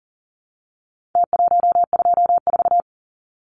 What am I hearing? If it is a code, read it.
Morse code, heard as a keyed tone, about 40 wpm, 700 Hz. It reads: T124
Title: Numbers - Contest Style - 4-Digit 40wpm